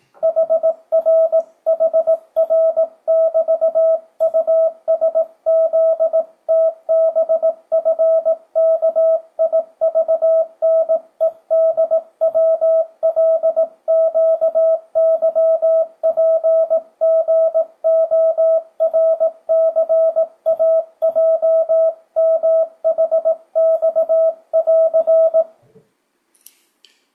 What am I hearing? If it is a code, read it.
Morse code, heard as a keyed tone, about 18 wpm, 650 Hz. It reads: HRHR<BT>USZTBFKIVNEDWLQYPGORCAJMHX<AR>
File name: 2025cw / speedup / 18wpm